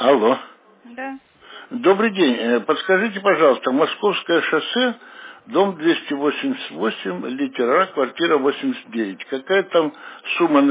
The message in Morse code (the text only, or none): none